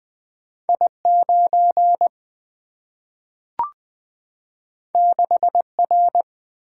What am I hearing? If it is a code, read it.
Morse code, heard as a keyed tone, about 20 wpm, 700 Hz. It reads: I9 6R
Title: Single-Letter Single-Number 20wpm